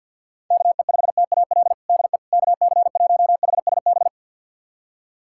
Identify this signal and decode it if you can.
Morse code, heard as a keyed tone, about 49 wpm, 690 Hz. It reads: KE5TAL DE KC1HSB
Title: USA Callsign DE Callsign - 6 Characters - 50wpm